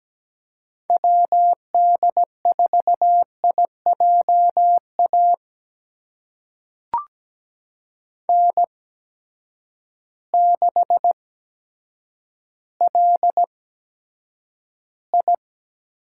Morse code, heard as a keyed tone, about 17 wpm, spaced as for 6 wpm, 700 Hz. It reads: WD4IJ AN6LI
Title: USA Call Signs - 8x Character Spacing - 17wpm